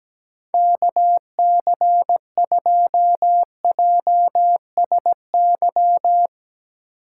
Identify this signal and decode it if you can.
Morse code, heard as a keyed tone, about 17 wpm, 700 Hz. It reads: KC2JSY
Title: Speed Racing - USA Call Signs 17wpm